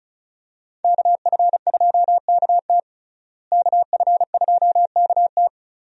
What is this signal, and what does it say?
Morse code, heard as a keyed tone, about 34 wpm, 700 Hz. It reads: KF2KT KF2KT
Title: USA Call Signs - 5 Characters - Repeated 2x - 35wpm